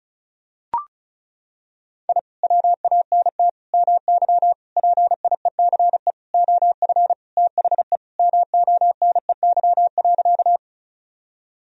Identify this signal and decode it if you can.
Morse code, heard as a keyed tone, about 34 wpm, 700 Hz. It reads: I WANT MY PIECE OF THE MONEY.
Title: Sentences from Top 1000 Words - Review Part 2 - 35wpm